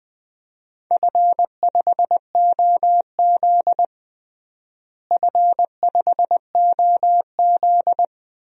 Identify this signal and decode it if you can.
Morse code, heard as a keyed tone, about 20 wpm, 700 Hz. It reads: F5OZ F5OZ
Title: DXCC-100 Call Signs - 4 Characters - Repeated 2x - 20wpm